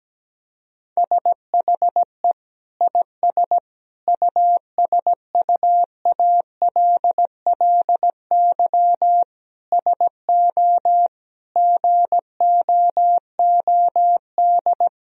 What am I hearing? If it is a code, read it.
Morse code, heard as a keyed tone, about 17 wpm, 700 Hz. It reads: SHE IS USUALLY SO GOOD
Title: Sentences from Top 1000 Words - Review Part 3 - 17wpm